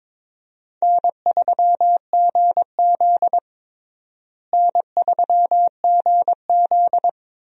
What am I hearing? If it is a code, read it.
Morse code, heard as a keyed tone, about 22 wpm, 700 Hz. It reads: N3GZ N3GZ
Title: USA Call Signs - 4 Characters - Repeated 3x - 22wpm